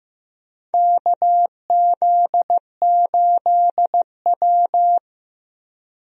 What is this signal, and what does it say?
Morse code, heard as a keyed tone, about 15 wpm, 700 Hz. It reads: KZ8W
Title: USA Call Signs - 4 Characters - Repeated 3x - 15wpm